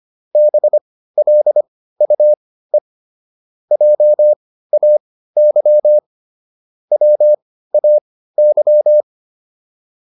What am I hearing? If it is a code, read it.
Morse code, heard as a keyed tone, about 25 wpm, spaced as for 15 wpm, 600 Hz. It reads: BLUE JAY WAY
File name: PR305 15